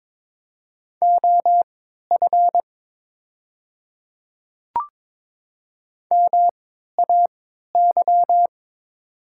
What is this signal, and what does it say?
Morse code, heard as a keyed tone, about 23 wpm, spaced as for 13 wpm, 700 Hz. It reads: OF MAY
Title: 3x Character Spacing - Top 100 Words - 22wpm